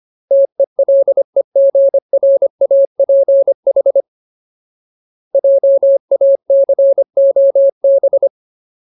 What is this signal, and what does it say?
Morse code, heard as a keyed tone, about 25 wpm, 550 Hz. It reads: TELEGRAPH JACOB
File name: Ultimate Word List / 25WPM Ultimate